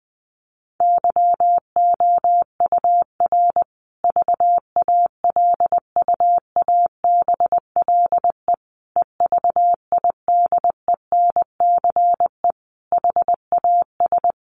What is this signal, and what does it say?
Morse code, heard as a keyed tone, about 20 wpm, 700 Hz. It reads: YOUR VALUABLE EVIDENCE HAS
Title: Sentences from Top 1300 Words 20wpm